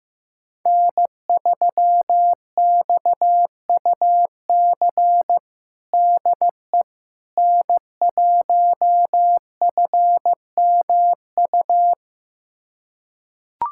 Morse code, heard as a keyed tone, about 15 wpm, 700 Hz. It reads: N3XUC DE N1FMU
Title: USA Callsign DE Callsign - 5 Characters - 15wpm